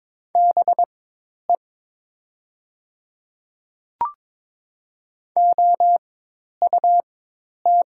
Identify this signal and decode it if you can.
Morse code, heard as a keyed tone, about 22 wpm, spaced as for 10 wpm, 700 Hz. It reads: BE OUT
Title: 4x Character Spacing - Top 100 Words - 22wpm